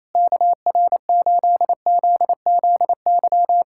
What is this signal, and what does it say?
Morse code, heard as a keyed tone, about 28 wpm, 700 Hz. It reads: KR8ZZY
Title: SOTA QSO Practice 28wpm